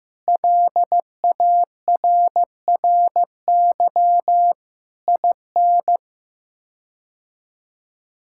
Morse code, heard as a keyed tone, about 15 wpm, 700 Hz. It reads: LARRY IN
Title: K1USN SST QSO Practice - 15wpm